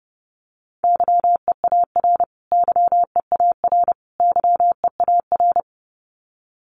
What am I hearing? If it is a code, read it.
Morse code, heard as a keyed tone, about 30 wpm, 700 Hz. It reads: YEAR YEAR YEAR
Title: Top 200 Words Repeated 30wpm